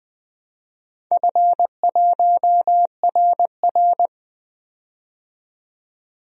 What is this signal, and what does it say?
Morse code, heard as a keyed tone, about 20 wpm, 700 Hz. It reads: F1RR
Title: DXCC-100 Call Signs - 4 Characters - 20wpm